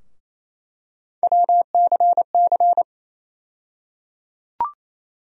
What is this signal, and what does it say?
Morse code, heard as a keyed tone, about 28 wpm, 700 Hz. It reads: WCC
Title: USA Call Sign Postfixes - 3 Letters 28wpm